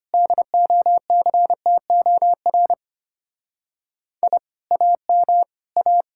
Morse code, heard as a keyed tone, about 31 wpm, 700 Hz. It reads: DOCTOR I AM A
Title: Speed-Racing Common QSO Phrases - 25wpm